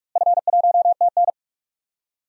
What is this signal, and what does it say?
Morse code, heard as a keyed tone, about 43 wpm, 700 Hz. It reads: F1TN
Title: DXCC-100 Call Signs - 4 Characters - Repeated 2x - 45wpm